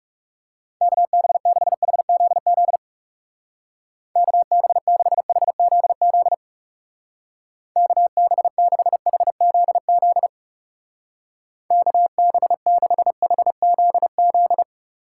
Morse code, heard as a keyed tone, about 37 wpm, 700 Hz. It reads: KB6HZZ KB6HZZ KB6HZZ KB6HZZ
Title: Speed Racing - USA 2x3 Call Signs 30wpm